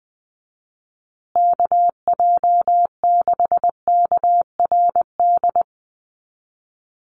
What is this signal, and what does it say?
Morse code, heard as a keyed tone, about 20 wpm, 700 Hz. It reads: KJ6KRD
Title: USA Call Signs Repeated 2x 20wpm